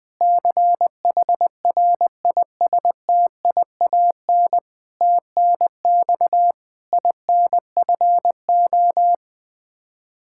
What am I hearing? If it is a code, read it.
Morse code, heard as a keyed tone, about 20 wpm, 700 Hz. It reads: CHRISTIAN TNX INFO
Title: K1USN SST QSO Practice 20wpm